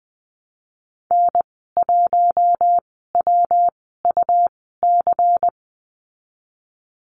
Morse code, heard as a keyed tone, about 20 wpm, spaced as for 15 wpm, 700 Hz. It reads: N1WUC
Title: 161 Lesson 26g - Letter g - Call Signs 20wpm